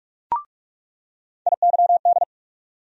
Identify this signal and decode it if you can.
Morse code, heard as a keyed tone, about 46 wpm, 700 Hz. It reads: IYD
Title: USA Call Sign Postfixes - 3 Letters 45wpm